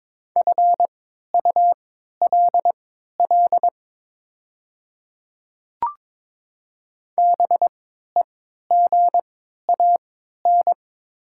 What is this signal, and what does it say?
Morse code, heard as a keyed tone, about 22 wpm, spaced as for 13 wpm, 700 Hz. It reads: FULL BEGAN
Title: 3x Character Spacing - Top 500 Words - 22wpm